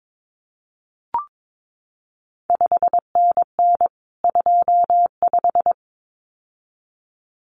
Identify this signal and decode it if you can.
Morse code, heard as a keyed tone, about 22 wpm, 700 Hz. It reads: 5NN 25
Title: CQ WW DX Contest Exchange - 22wpm